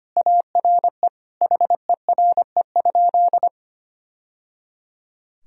Morse code, heard as a keyed tone, about 25 wpm, 700 Hz. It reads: ARE HERE?
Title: Sentences from Top 1000 Words - Review Part 1 - 25wpm